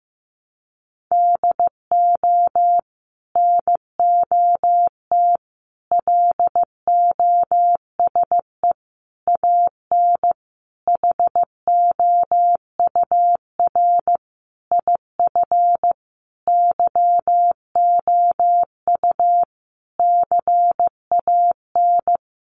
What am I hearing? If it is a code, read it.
Morse code, heard as a keyed tone, about 15 wpm, 700 Hz. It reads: DO NOT LOSE AN HOUR IF YOU CAN
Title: Sentences from Top 1100 Words 15wpm